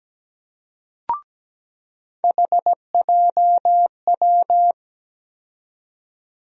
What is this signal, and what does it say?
Morse code, heard as a keyed tone, about 17 wpm, 700 Hz. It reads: HJW
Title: USA Call Sign Postfixes - 3 Letters 17wpm